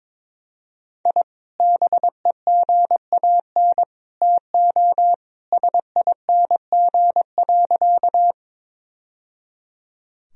Morse code, heard as a keyed tone, about 22 wpm, 700 Hz. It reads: I BEGAN TO SING.
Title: Sentences from Top 1000 Words - Review Part 2 - 22wpm